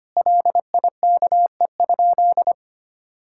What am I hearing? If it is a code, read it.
Morse code, heard as a keyed tone, about 25 wpm, 700 Hz. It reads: LIKE?
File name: practice / 25wpm / Sentences with 3-Word Phrases - 25wpm